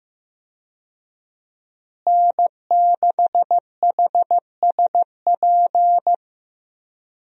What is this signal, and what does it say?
Morse code, heard as a keyed tone, about 15 wpm, 700 Hz. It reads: N6HSP
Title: USA Call Signs - 5 Characters - Repeated 2x - 15wpm